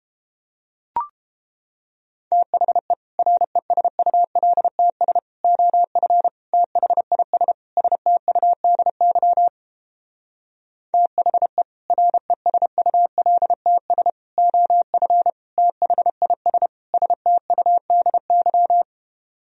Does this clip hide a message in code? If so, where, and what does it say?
Yes: Morse code from 2.3 to 18.8 s, heard as a keyed tone, about 29 wpm, 700 Hz. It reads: THE RESULTS OF THIS STUDY THE RESULTS OF THIS STUDY